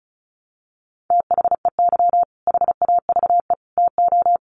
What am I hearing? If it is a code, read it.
Morse code, heard as a keyed tone, about 34 wpm, 700 Hz. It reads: THEY HAVE TO
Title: Sets of 3 Words 35wpm